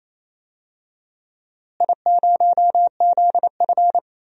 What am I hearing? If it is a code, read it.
Morse code, heard as a keyed tone, about 27 wpm, 700 Hz. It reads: I0ZF
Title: DXCC-100 Call Signs - 4 Characters - Repeated 2x - 28wpm